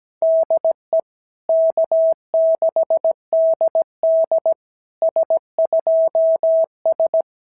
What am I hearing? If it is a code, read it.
Morse code, heard as a keyed tone, about 17 wpm, 650 Hz. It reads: DE K6DD S2S
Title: SOTA QSO Practice 17wpm